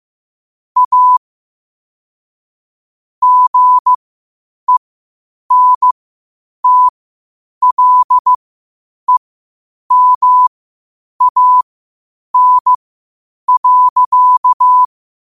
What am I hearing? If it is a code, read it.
Morse code, heard as a keyed tone, about 15 wpm, spaced as for 8 wpm, 1000 Hz. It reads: A GENTLEMAN.
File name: qotdfiles / 8wpm